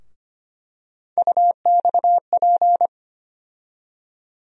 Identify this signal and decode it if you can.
Morse code, heard as a keyed tone, about 25 wpm, 700 Hz. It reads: UXP
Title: USA Call Sign Postfixes - 3 Letters 25wpm